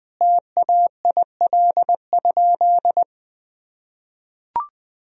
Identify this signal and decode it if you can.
Morse code, heard as a keyed tone, about 20 wpm, 700 Hz. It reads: TAIL?
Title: Sentences from Top 1000 Words - Review Part 3 - 20wpm